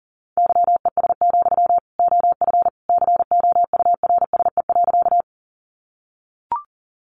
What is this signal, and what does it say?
Morse code, heard as a keyed tone, about 40 wpm, 700 Hz. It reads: YES, OF COURSE.
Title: Sentences from Top 500 Words 40wpm